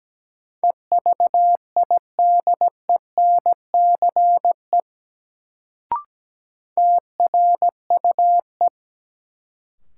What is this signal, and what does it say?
Morse code, heard as a keyed tone, about 17 wpm, 700 Hz. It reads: EVIDENCE TRUE
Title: Sets of 5 Words Encore - New Word Drill - 17wpm